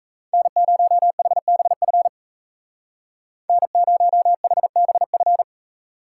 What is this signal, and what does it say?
Morse code, heard as a keyed tone, about 36 wpm, 700 Hz. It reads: N0HBF N0HBF